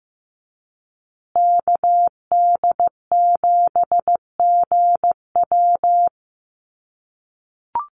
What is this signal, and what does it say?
Morse code, heard as a keyed tone, about 15 wpm, 700 Hz. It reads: KD7GW
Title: USA Call Signs Repeated 2x 15wpm